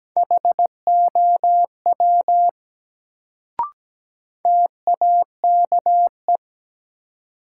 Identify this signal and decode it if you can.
Morse code, heard as a keyed tone, about 17 wpm, 700 Hz. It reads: HOW TAKE
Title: Sets of 2 Words - New Word Drill - 17wpm